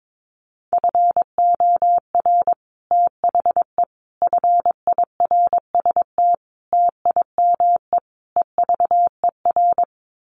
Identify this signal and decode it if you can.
Morse code, heard as a keyed tone, about 22 wpm, 700 Hz. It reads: FOR THE FIRST TIME EVER
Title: Sets of 5 Words Encore2 22wpm